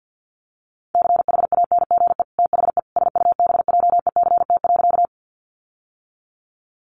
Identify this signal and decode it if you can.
Morse code, heard as a keyed tone, about 50 wpm, 700 Hz. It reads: CHANGE THE SUBJECT.
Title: Sentences from Top 800 Words 50wpm